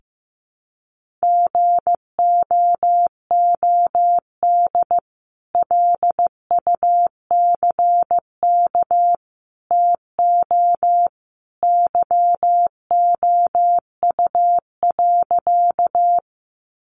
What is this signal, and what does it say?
Morse code, heard as a keyed tone, about 15 wpm, 700 Hz. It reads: GOOD LUCK TO YOU.
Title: Sentences from Top 1500 Words 15wpm